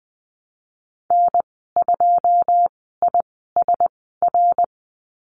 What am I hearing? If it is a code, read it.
Morse code, heard as a keyed tone, about 20 wpm, spaced as for 15 wpm, 700 Hz. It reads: N2ISR
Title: 082 Lesson 14g - Number 2 - Call Signs 20wpm